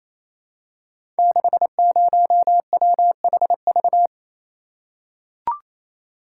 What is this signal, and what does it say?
Morse code, heard as a keyed tone, about 28 wpm, 700 Hz. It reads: 60WHV
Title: Code Groups - 5 Letter:Numbers - 28wpm